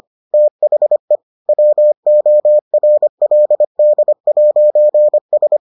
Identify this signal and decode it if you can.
Morse code, heard as a keyed tone, about 25 wpm, 600 Hz. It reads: THE WORLD'S